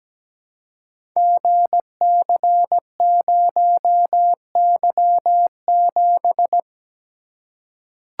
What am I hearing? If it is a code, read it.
Morse code, heard as a keyed tone, about 17 wpm, 700 Hz. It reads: GC0Y7
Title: Code Groups - 5 Letter:Numbers - 17wpm